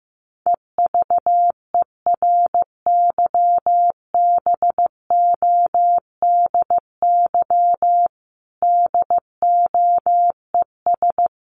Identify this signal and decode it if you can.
Morse code, heard as a keyed tone, about 15 wpm, 700 Hz. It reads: EVERYBODY DOES
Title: Sentences from Top 1400 Words 15wpm